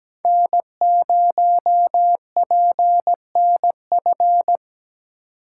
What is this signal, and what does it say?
Morse code, heard as a keyed tone, about 17 wpm, 700 Hz. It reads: N0PNF
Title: USA Call Signs - 5 Characters - 2x Character Spacing - 17wpm